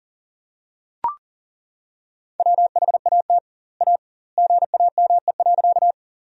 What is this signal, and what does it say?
Morse code, heard as a keyed tone, about 41 wpm, 700 Hz. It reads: WHAT A GAME.
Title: 2x Word-Spacing - Sentences from Top 500 Words - 40wpm